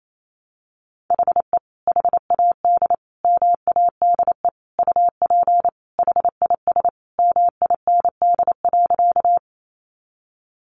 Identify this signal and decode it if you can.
Morse code, heard as a keyed tone, about 28 wpm, 700 Hz. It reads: HE HAD MADE UP HIS MIND.
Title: Sentences from Top 500 Words 28wpm